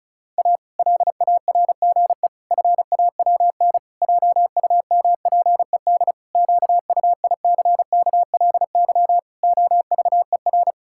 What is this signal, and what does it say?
Morse code, heard as a keyed tone, about 36 wpm, 700 Hz. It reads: A LARGE FAWN JUMPED QUICKLY OVER
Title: Pangram Challenge v2 - 35wpm